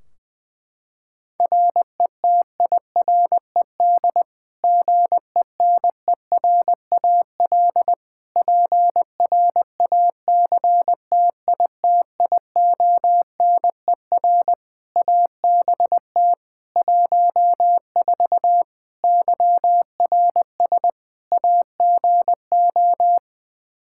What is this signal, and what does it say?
Morse code, heard as a keyed tone, about 20 wpm, 700 Hz. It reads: RETIRED GENERAL PRACTITIONER ABT 14 YRS AGO
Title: Practice QSO - Exchange 3 - More Info - 20wpm